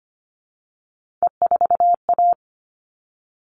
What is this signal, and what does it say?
Morse code, heard as a keyed tone, about 25 wpm, 700 Hz. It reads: E4A
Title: All Cut Numbers - 3-Digit 25wpm